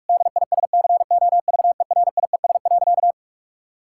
Morse code, heard as a keyed tone, about 46 wpm, 700 Hz. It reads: DISCOVERIES.
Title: Pangram Challenge v2 - 45wpm